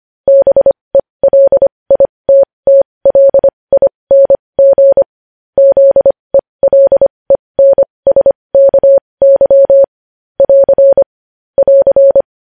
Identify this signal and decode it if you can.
Morse code, heard as a keyed tone, about 25 wpm, spaced as for 20 wpm, 550 Hz. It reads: BELITTLING ZELENSKY <AR> <AR>
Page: News Headlines in Morse Code at 20 WPM